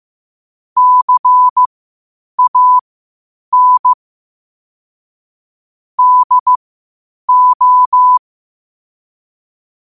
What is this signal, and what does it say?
Morse code, heard as a keyed tone, about 15 wpm, spaced as for 8 wpm, 1000 Hz. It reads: CAN DO
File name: qotdfiles / 8wpm